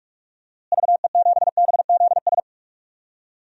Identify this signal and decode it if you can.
Morse code, heard as a keyed tone, about 46 wpm, 700 Hz. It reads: VE7BZS
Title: DXCC-100 Call Signs - 6 Characters - Repeated 2x - 45wpm